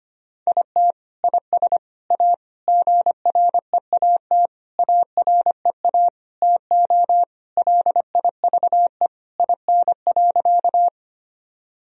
Morse code, heard as a keyed tone, about 25 wpm, 700 Hz. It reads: IT IS A GREAT AREA TO LIVE IN.
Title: Sentences from Top 1000 Words - Review Part 2 - 25wpm